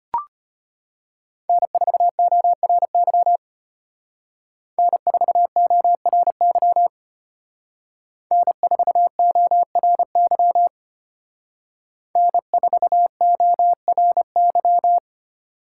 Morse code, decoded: N4ORY N4ORY N4ORY N4ORY

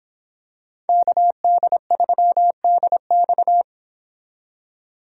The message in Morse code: KD3DX

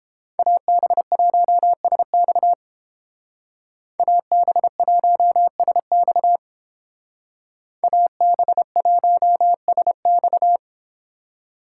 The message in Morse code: AB1SX AB1SX AB1SX

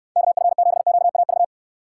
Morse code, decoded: KF7QAV

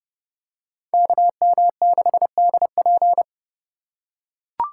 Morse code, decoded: KM6DP

